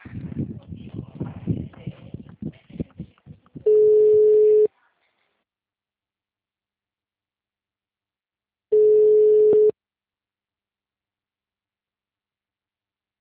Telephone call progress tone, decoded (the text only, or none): ringback tone